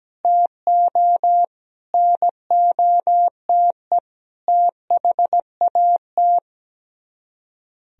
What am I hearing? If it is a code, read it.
Morse code, heard as a keyed tone, about 17 wpm, 700 Hz. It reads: TO NOTE THAT
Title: Speed-Racing - Sets of 5 Words 17wpm